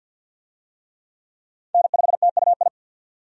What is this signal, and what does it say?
Morse code, heard as a keyed tone, about 45 wpm, 700 Hz. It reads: N5TUI
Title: USA Call Signs - 5 Characters - 50wpm